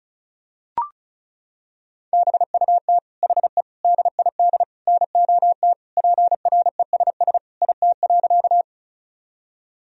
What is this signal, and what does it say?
Morse code, heard as a keyed tone, about 36 wpm, 700 Hz. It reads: BUT HE DID NOT PRESS IT.
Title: Sentences from Top 1000 Words - Review Part 1 - 35wpm